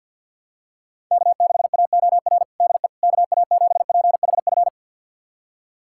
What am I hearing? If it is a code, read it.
Morse code, heard as a keyed tone, about 52 wpm, 700 Hz. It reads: K6AOR DE KA7PHF
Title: USA Callsign DE Callsign 50wpm